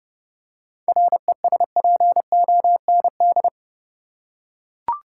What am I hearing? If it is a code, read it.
Morse code, heard as a keyed tone, about 30 wpm, 700 Hz. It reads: RESPOND
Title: Seven Letter Words - 30wpm